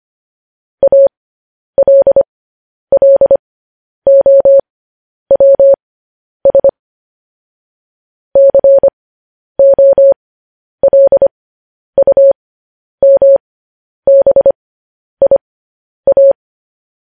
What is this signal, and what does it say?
Morse code, heard as a keyed tone, about 25 wpm, spaced as for 10 wpm, 550 Hz. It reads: ALLOWS COLUMBIA
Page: news headlines in morse code at 10 wpm